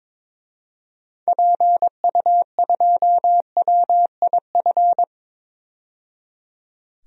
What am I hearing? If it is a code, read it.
Morse code, heard as a keyed tone, about 22 wpm, 700 Hz. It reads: PU2WIF
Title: DXCC-100 Call Signs - 6 Characters - 22wpm